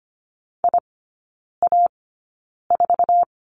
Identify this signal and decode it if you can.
Morse code, heard as a keyed tone, about 25 wpm, spaced as for 9 wpm, 690 Hz. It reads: IA4
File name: Lesson 15e - Number 5 - Three Characters 25wpm